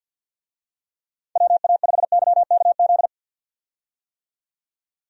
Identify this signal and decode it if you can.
Morse code, heard as a keyed tone, about 50 wpm, 700 Hz. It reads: WA5YKZ
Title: USA Call Signs - 6 Characters - 50wpm